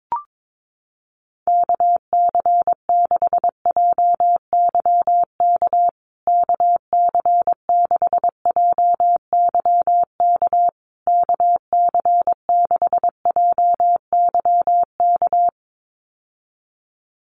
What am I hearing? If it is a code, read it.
Morse code, heard as a keyed tone, about 22 wpm, 700 Hz. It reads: KC6JYK KC6JYK KC6JYK
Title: USA Call Signs Repeated 3x 22wpm